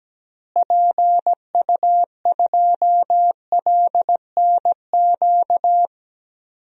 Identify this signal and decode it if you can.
Morse code, heard as a keyed tone, about 17 wpm, 700 Hz. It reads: PU2LNQ
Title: DXCC-100 Call Signs - 6 Characters - Repeated 2x - 17wpm